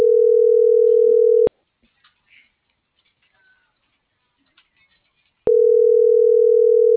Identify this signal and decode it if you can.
Ringback tone (North America), 440 + 480 Hz.